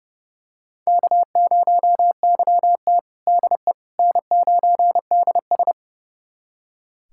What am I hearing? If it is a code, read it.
Morse code, heard as a keyed tone, about 30 wpm, 700 Hz. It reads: K0YT DE N9DS
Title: USA Callsign DE Callsign - 4 Characters - 30wpm